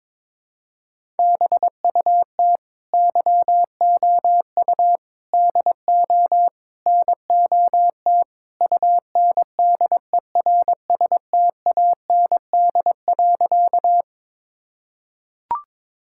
Morse code, heard as a keyed tone, about 21 wpm, 700 Hz. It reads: BUT YOU DO NOT UNDERSTAND.
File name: practice / 22wpm / Sentences with 4-Word Phrases - 22wpm